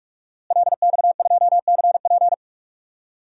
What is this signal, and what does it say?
Morse code, heard as a keyed tone, about 41 wpm, 700 Hz. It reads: LX2CP
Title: DXCC-100 Call Signs - 5 Characters - Repeated 2x - 45wpm